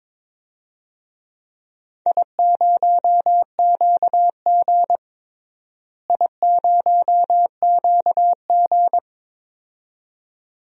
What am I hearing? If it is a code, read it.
Morse code, heard as a keyed tone, about 22 wpm, 700 Hz. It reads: I0QG I0QG
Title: DXCC-100 Call Signs - 4 Characters - Repeated 2x - 22wpm